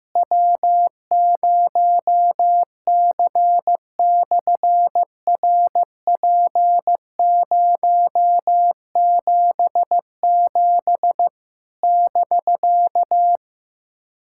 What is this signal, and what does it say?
Morse code, heard as a keyed tone, about 15 wpm, 700 Hz. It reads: W0C/RP077 <BK>
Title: SOTA QSO Practice 15wpm